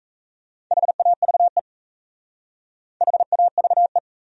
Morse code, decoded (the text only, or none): HAVE HAVE